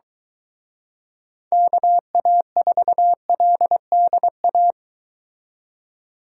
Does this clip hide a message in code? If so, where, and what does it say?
Yes: Morse code from 1.5 to 4.7 s, heard as a keyed tone, about 22 wpm, 700 Hz. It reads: KA4LDA